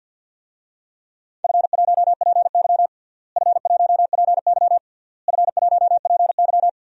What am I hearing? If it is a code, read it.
Morse code, heard as a keyed tone, about 51 wpm, 700 Hz. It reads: F1PY F1PY F1PY
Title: DXCC-100 Call Signs - 4 Characters - Repeated 3x - 50wpm